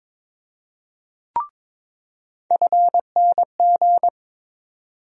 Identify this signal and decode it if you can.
Morse code, heard as a keyed tone, about 21 wpm, 700 Hz. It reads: FNG
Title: USA Call Sign Postfixes - 3 Letters 22wpm